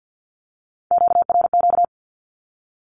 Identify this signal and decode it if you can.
Morse code, heard as a keyed tone, about 48 wpm, 700 Hz. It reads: QRQ
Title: Q-Codes Repeated 3x 50wpm